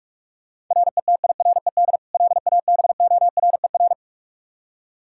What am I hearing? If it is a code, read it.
Morse code, heard as a keyed tone, about 46 wpm, 700 Hz. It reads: RETIRED LABORER
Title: Practice QSO - Exchange 3 - More Info - 45wpm